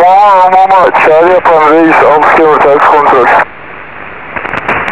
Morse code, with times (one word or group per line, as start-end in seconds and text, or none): none